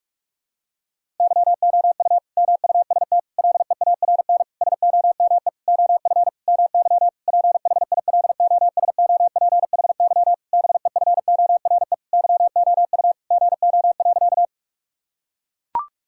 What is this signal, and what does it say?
Morse code, heard as a keyed tone, about 41 wpm, 700 Hz. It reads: YOU MUST LEARN SOME OF MY PHILOSOPHY BEFORE YOU GO.
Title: Sentences from Top 1000 Words - Review Part 6 - 45wpm